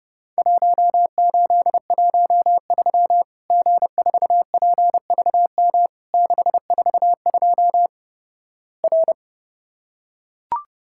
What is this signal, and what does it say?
Morse code, heard as a keyed tone, about 32 wpm, 700 Hz. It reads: 1813 G4PVM 642 R
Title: Worked All Europe DX Contest - QTC Exchange - 30wpm